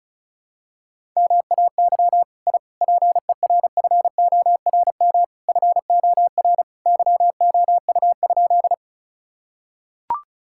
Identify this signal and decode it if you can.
Morse code, heard as a keyed tone, about 34 wpm, 700 Hz. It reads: MAY I PERFORM FOR YOU?
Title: Sentences from Top 1000 Words - Review Part 5 - 35wpm